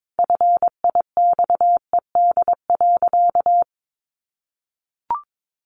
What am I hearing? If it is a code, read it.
Morse code, heard as a keyed tone, about 22 wpm, 700 Hz. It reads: FIXED.
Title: Sentences from Top 1100 Words 22wpm